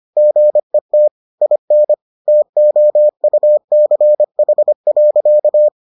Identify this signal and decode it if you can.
Morse code, heard as a keyed tone, about 25 wpm, 600 Hz. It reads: GET IN TOUCH.